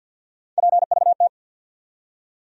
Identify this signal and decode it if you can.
Morse code, heard as a keyed tone, about 52 wpm, 700 Hz. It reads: PVT
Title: USA Call Sign Postfixes - 3 Letters 50wpm